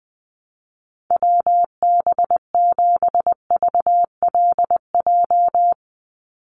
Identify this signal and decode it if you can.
Morse code, heard as a keyed tone, about 20 wpm, 700 Hz. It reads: WB7VLJ
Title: USA Call Signs 20wpm